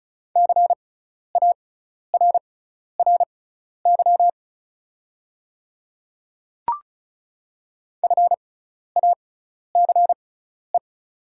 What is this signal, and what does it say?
Morse code, heard as a keyed tone, about 36 wpm, spaced as for 12 wpm, 690 Hz. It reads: CARRY FACE
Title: 6x Character Spacing - Top 300 Words - 35wpm